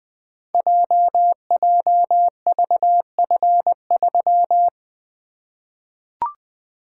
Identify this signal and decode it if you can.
Morse code, heard as a keyed tone, about 20 wpm, 700 Hz. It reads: JJVF3
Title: Code Groups - 5 Letter:Numbers - 20wpm